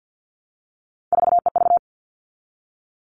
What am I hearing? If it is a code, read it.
Morse code, heard as a keyed tone, about 50 wpm, 700 Hz. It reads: 4EV
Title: All Cut Numbers - 3-Digit 50wpm